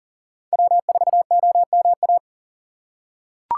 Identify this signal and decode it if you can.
Morse code, heard as a keyed tone, about 40 wpm, 700 Hz. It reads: W4OMA
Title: USA Call Signs - 5 Characters - 3x Character Spacing - 40wpm